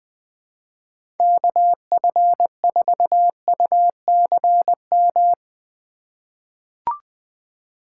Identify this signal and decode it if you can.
Morse code, heard as a keyed tone, about 20 wpm, 700 Hz. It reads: KF4UCM
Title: USA Call Signs - 6 Characters - 4x Character Spacing - 20wpm